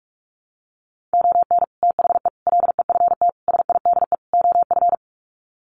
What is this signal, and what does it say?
Morse code, heard as a keyed tone, about 44 wpm, 700 Hz. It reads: ON THE LEFT SIDE OF
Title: Sets of 5 Words Encore2 45wpm